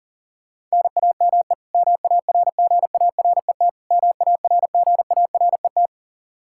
Morse code, heard as a keyed tone, about 37 wpm, 700 Hz. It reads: NAME MARGARET MARGARET
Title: Practice QSO - Exchange 1 - Beginning - 40wpm